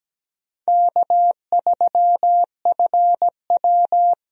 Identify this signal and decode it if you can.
Morse code, heard as a keyed tone, about 17 wpm, 700 Hz. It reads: K3FW
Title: USA Call Signs - 4 Characters - 17wpm